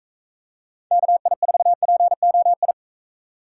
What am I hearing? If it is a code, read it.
Morse code, heard as a keyed tone, about 41 wpm, 700 Hz. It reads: KI4POI